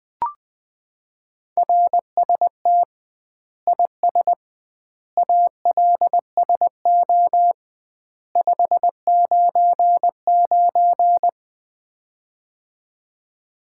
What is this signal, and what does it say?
Morse code, heard as a keyed tone, about 20 wpm, 700 Hz. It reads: RST IS ALSO 599
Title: Common QSO Phrases - 2x Word-Spacing - 20wpm